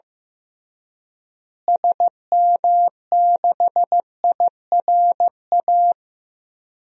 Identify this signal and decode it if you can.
Morse code, heard as a keyed tone, about 15 wpm, 700 Hz. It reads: SM6IRA
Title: DXCC-100 Call Signs - 6 Characters - Repeated 3x - 15wpm